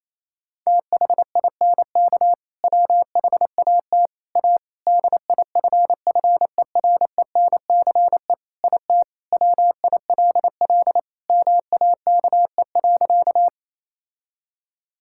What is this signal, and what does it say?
Morse code, heard as a keyed tone, about 28 wpm, 700 Hz. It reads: THINK WHAT A DIFFERENCE IT WILL MAKE.
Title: Sentences from Top 1000 Words - Review Part 5 - 28wpm